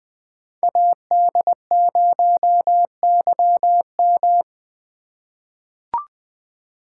Morse code, heard as a keyed tone, about 20 wpm, 700 Hz. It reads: AD0YM
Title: 2023 SOTA Chaser Call Signs - 20wpm